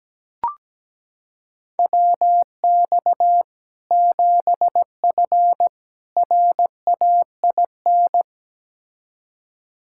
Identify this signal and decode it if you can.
Morse code, heard as a keyed tone, about 17 wpm, 700 Hz. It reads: WX 7F RAIN
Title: K1USN SST QSO Practice 17wpm